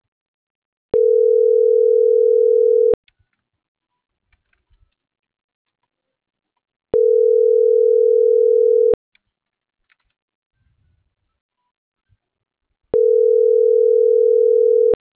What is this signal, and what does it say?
Ringback tone (North America), 440 + 480 Hz.